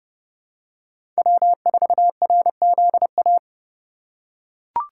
Morse code, heard as a keyed tone, about 32 wpm, 700 Hz. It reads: W4RZA
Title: USA Call Signs - 5 Characters - 2x Character Spacing - 30wpm